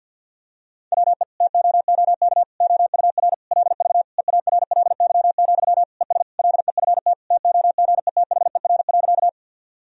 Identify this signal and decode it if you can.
Morse code, heard as a keyed tone, about 50 wpm, 700 Hz. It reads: WE TOOK OUR LV EARLY, ES LEFT TOGETHER.
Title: Sentences from Top 400 Words and Abbreviations 50wpm